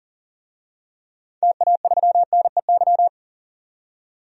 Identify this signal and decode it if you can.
Morse code, heard as a keyed tone, about 39 wpm, 700 Hz. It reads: TA3NEY
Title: DXCC-100 Call Signs - 6 Characters - 40wpm